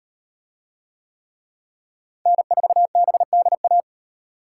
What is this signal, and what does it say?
Morse code, heard as a keyed tone, about 36 wpm, 700 Hz. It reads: N4BDA